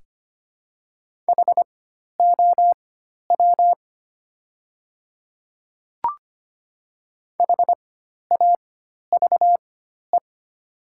Morse code, heard as a keyed tone, about 25 wpm, spaced as for 12 wpm, 700 Hz. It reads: HOW HAVE
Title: 4x Character Spacing - Top 200 Words - 25wpm